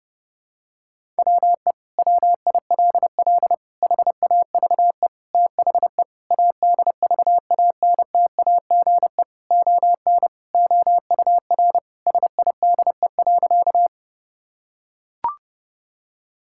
Morse code, heard as a keyed tone, about 30 wpm, 700 Hz. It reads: WE WILL HAVE THE ADVANTAGE ON OUR SIDE.
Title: Sentences from Top 1000 Words - Review Part 6 - 30wpm